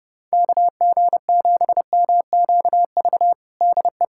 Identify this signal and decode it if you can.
Morse code, heard as a keyed tone, about 30 wpm, 700 Hz. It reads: KG7MQV DE
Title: USA Callsign DE Callsign 30wpm